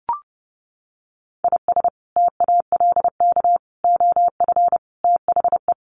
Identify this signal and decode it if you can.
Morse code, heard as a keyed tone, about 30 wpm, 700 Hz. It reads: IS TALK OF THE
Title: Sets of 5 Words Encore2 30wpm